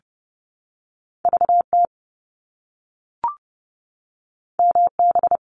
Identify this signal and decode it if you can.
Morse code, heard as a keyed tone, about 30 wpm, 700 Hz. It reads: VT MB
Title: ARRL Sections 30wpm